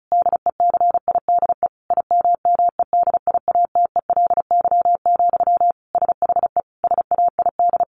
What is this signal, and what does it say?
Morse code, heard as a keyed tone, about 34 wpm, 700 Hz. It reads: DECIDE IMMEDIATELY, SHE SAID
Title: Sentences from Top 1500 Words 35wpm